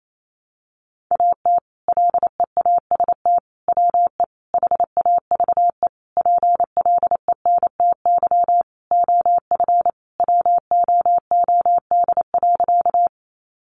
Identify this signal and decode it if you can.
Morse code, heard as a keyed tone, about 28 wpm, 700 Hz. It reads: AT LEAST WE HAVE PLENTY OF WOOD.
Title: Sentences from Top 1300 Words 28wpm